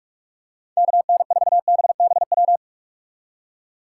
Morse code, heard as a keyed tone, about 44 wpm, 700 Hz. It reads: KN4BBW